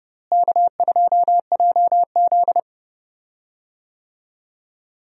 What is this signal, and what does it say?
Morse code, heard as a keyed tone, about 30 wpm, 700 Hz. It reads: K2JZ
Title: USA Call Signs - 4 Characters - Repeated 3x - 30wpm